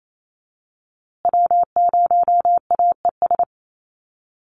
Morse code, heard as a keyed tone, about 28 wpm, 700 Hz. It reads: W0AES
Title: USA Call Signs Repeated 3x 28wpm